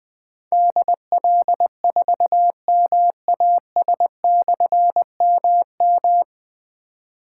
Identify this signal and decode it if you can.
Morse code, heard as a keyed tone, about 20 wpm, 700 Hz. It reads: DL4MAS/MM
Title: Difficult Call Signs 20wpm